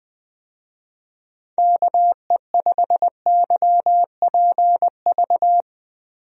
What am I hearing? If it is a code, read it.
Morse code, heard as a keyed tone, about 20 wpm, 700 Hz. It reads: KE5YPV
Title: USA Call Signs - 6 Characters - 20wpm